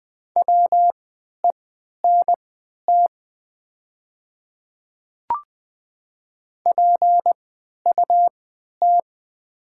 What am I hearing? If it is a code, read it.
Morse code, heard as a keyed tone, about 20 wpm, spaced as for 11 wpm, 700 Hz. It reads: WENT PUT
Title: 3x Character Spacing - Top 200 Words - 20wpm